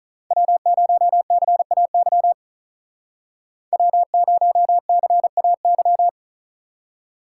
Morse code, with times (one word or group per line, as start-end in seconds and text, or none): 0.3-2.3 W0CAY
3.7-6.1 W0CAY